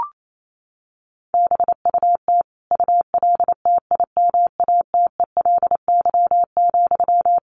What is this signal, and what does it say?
Morse code, heard as a keyed tone, about 28 wpm, 700 Hz. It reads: BUT ULTIMATELY,
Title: Sentences from Top 1200 Words 28wpm